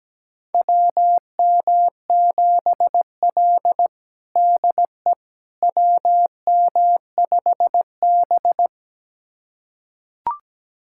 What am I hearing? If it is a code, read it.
Morse code, heard as a keyed tone, about 17 wpm, 700 Hz. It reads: WM7L DE WM5B
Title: USA Callsign DE Callsign - 4 Characters - 17wpm